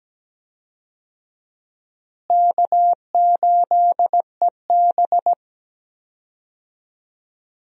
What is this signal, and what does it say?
Morse code, heard as a keyed tone, about 17 wpm, 700 Hz. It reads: K8EB
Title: USA Call Signs - 4 Characters - Repeated 2x - 17wpm